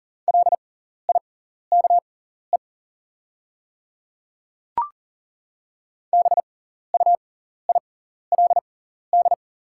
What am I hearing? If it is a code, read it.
Morse code, heard as a keyed tone, about 37 wpm, spaced as for 14 wpm, 700 Hz. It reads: LIKE BUILD
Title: 6x Character Spacing - Top 300 Words - 40wpm